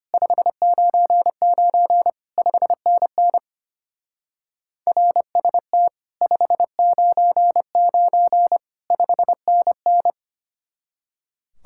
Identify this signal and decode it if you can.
Morse code, heard as a keyed tone, about 30 wpm, 700 Hz. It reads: 599 5NN RST 599 5NN
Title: Speed-Racing Common QSO Phrases - 25wpm